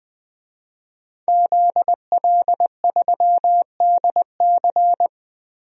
Morse code, heard as a keyed tone, about 21 wpm, 700 Hz. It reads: ZL3DC
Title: DXCC-100 Call Signs - 5 Characters - Repeated 3x - 20wpm